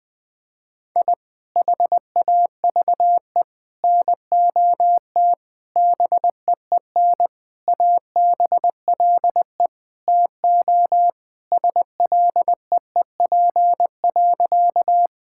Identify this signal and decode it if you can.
Morse code, heard as a keyed tone, about 20 wpm, 700 Hz. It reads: I HAVE NOT BEEN ABLE TO SLEEP.
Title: Sentences with 5-Word Phrases - 20wpm